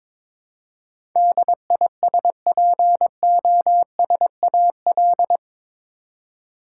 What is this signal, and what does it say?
Morse code, heard as a keyed tone, about 22 wpm, 700 Hz. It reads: DISPOSAL
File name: practice / 22wpm / Eight Letter Words - 22wpm